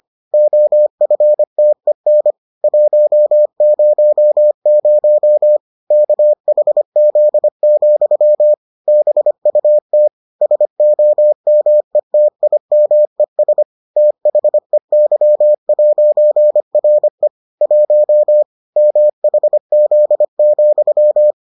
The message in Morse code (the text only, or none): OFTEN 100 KHZ, BUT SOMETIMES THEY'RE 1 MHZ,